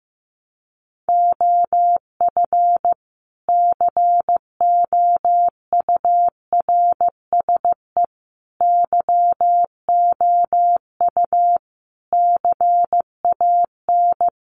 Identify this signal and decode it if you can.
Morse code, heard as a keyed tone, about 15 wpm, 700 Hz. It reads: OF COURSE YOU CAN
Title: Sentences from Top 1300 Words 15wpm